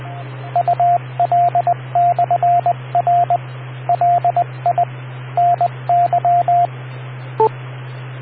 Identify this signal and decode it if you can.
Morse code, heard as a keyed tone, about 20 wpm, 700 Hz, with radio noise. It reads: UL/R LI NY